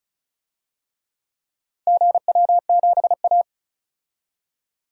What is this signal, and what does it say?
Morse code, heard as a keyed tone, about 36 wpm, 700 Hz. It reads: GW7A
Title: DXCC-100 Call Signs - 4 Characters - 35wpm